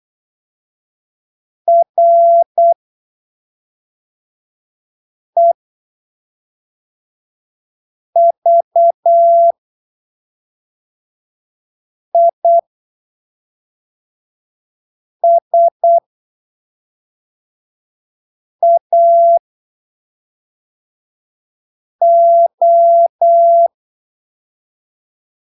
Morse code, heard as a keyed tone, about 8 wpm, spaced as for 3 wpm, 680 Hz. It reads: REVISAO
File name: arquivos / texto / texto 22 08ppm